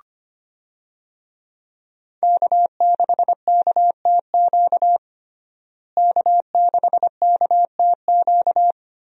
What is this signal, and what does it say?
Morse code, heard as a keyed tone, about 25 wpm, 700 Hz. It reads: K6KTQ K6KTQ
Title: USA Call Signs - 5 Characters - Repeated 3x - 25wpm